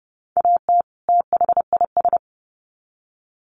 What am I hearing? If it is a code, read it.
Morse code, heard as a keyed tone, about 30 wpm, 700 Hz. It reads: AT THIS
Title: Sets of 2 Words 30wpm